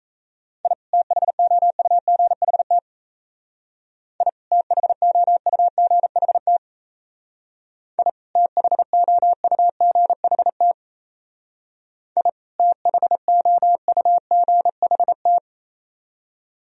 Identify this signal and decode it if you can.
Morse code, heard as a keyed tone, about 34 wpm, 690 Hz. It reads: I THOUGHT I THOUGHT I THOUGHT I THOUGHT
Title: Speed-Racing - Sets of 2 Words 28wpm